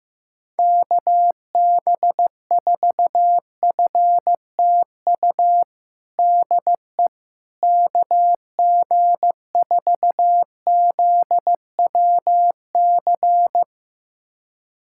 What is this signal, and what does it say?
Morse code, heard as a keyed tone, about 15 wpm, 700 Hz. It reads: KB4FTU DE KG4ZWC
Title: USA Callsign DE Callsign - 6 Characters - 15wpm